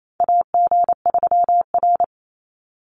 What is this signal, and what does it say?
Morse code, heard as a keyed tone, about 28 wpm, 700 Hz. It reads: AG3R
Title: Call Signs Repeated 3x - 28wpm